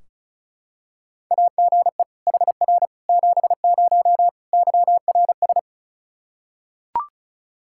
Morse code, heard as a keyed tone, about 34 wpm, 700 Hz. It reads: AGE HR 70 YRS
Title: Practice QSO - Exchange 3 - More Info - 35wpm